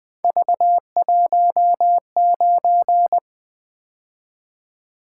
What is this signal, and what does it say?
Morse code, heard as a keyed tone, about 20 wpm, 700 Hz. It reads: V19
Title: Three Letter-Number Combinations - 20wpm